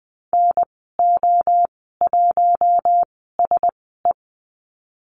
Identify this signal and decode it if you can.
Morse code, heard as a keyed tone, about 20 wpm, spaced as for 15 wpm, 700 Hz. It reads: NO1SE
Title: 048 Lesson 9g - Number 4 - Call Signs 20wpm